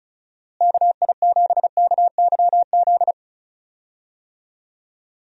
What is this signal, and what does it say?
Morse code, heard as a keyed tone, about 34 wpm, 700 Hz. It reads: KI7KYZ
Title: USA Call Signs - 6 Characters - Repeated 2x - 35wpm